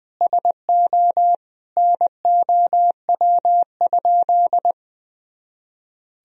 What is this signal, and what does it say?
Morse code, heard as a keyed tone, about 20 wpm, 700 Hz. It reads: SO NOW?
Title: Sentences from Top 200 Words and Abbreviations 20wpm